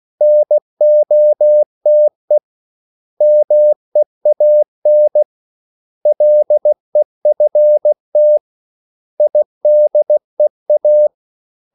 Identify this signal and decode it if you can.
Morse code, heard as a keyed tone, about 16 wpm, 600 Hz. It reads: NOTE MEAN LEFT IDEA
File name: CW / Practice / 500MostCommonEnglishWords16